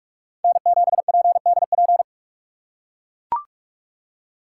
Morse code, heard as a keyed tone, about 42 wpm, 700 Hz. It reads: N7PDP
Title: USA Call Signs - 5 Characters - 2x Character Spacing - 45wpm